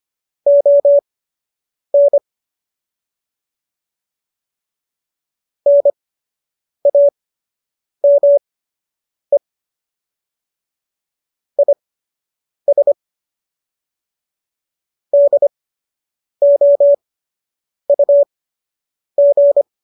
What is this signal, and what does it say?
Morse code, heard as a keyed tone, about 25 wpm, spaced as for 8 wpm, 570 Hz. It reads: ON NAME IS DOUG